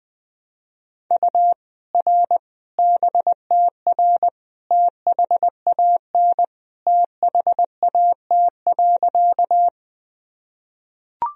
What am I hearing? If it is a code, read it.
Morse code, heard as a keyed tone, about 20 wpm, 700 Hz. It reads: U R BTR THAN THAT.
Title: Sentences from Top 400 Words and Abbreviations 20wpm